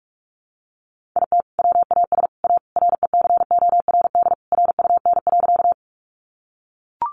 Morse code, heard as a keyed tone, about 44 wpm, 700 Hz. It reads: IT WAS A RECORD RUN.
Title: Sentences from Top 500 Words 45wpm